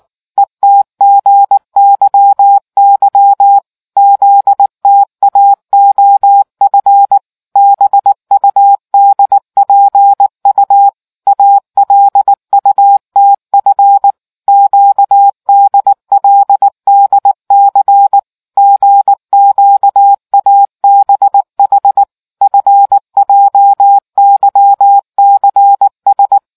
Morse code, heard as a keyed tone, about 19 wpm, 800 Hz. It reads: ETGYY ZTAOF BUDPU ALUTF QDLDC GQABH FJYCS